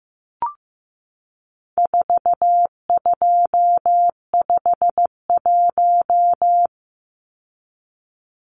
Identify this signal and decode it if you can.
Morse code, heard as a keyed tone, about 15 wpm, 700 Hz. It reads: 4251
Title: Numbers - 4-Digit 15wpm